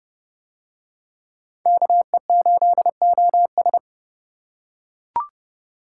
Morse code, heard as a keyed tone, about 28 wpm, 700 Hz. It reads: KE8OS